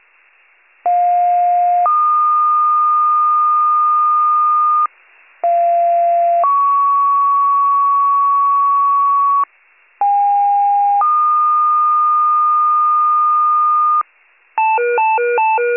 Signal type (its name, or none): two-tone page